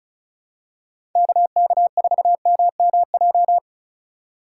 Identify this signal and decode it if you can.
Morse code, heard as a keyed tone, about 34 wpm, 700 Hz. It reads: KK4MMJ
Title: USA Call Signs - 6x Character Spacing - 35wpm